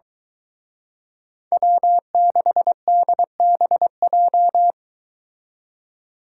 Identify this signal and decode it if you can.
Morse code, heard as a keyed tone, about 22 wpm, 700 Hz. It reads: W6DBJ